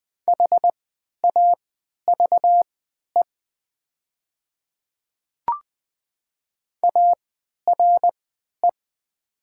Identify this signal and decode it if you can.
Morse code, heard as a keyed tone, about 20 wpm, spaced as for 11 wpm, 700 Hz. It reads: HAVE ARE
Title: 3x Character Spacing - Top 200 Words - 20wpm